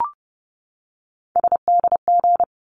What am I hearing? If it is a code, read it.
Morse code, heard as a keyed tone, about 30 wpm, 700 Hz. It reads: SDG
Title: ARRL Sections 30wpm